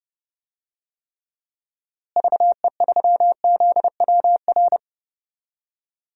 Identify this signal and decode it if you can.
Morse code, heard as a keyed tone, about 30 wpm, 700 Hz. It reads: VE3ZWR
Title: DXCC-100 Call Signs - 6 Characters - Repeated 3x - 30wpm